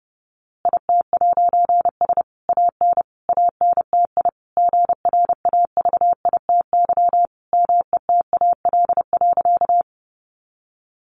Morse code, heard as a keyed tone, about 30 wpm, 700 Hz. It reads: IT'S AN ANTI GRAVITY METAL.
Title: A Sound Of Thunder 30wpm